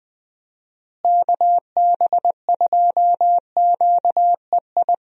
Morse code, heard as a keyed tone, about 21 wpm, 700 Hz. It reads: KB2QEI
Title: USA Call Signs - 6 Characters - Repeated 2x - 20wpm